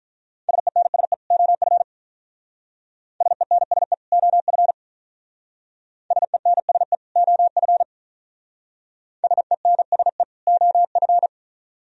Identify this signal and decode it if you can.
Morse code, heard as a keyed tone, about 45 wpm, 700 Hz. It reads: SENSE OF SENSE OF SENSE OF SENSE OF